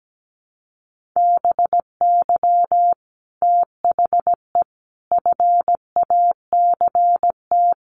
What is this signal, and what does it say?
Morse code, heard as a keyed tone, about 17 wpm, 700 Hz. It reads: BY THE FACT
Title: Sets of 5 Words Encore 17wpm